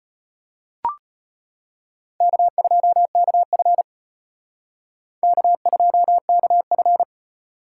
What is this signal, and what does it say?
Morse code, heard as a keyed tone, about 32 wpm, 700 Hz. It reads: K2KF K2KF